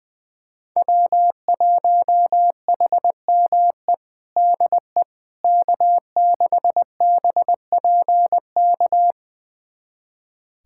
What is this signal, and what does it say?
Morse code, heard as a keyed tone, about 20 wpm, 700 Hz. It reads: W1HME DE K6BPK
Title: USA Callsign DE Callsign - 5 Characters - 20wpm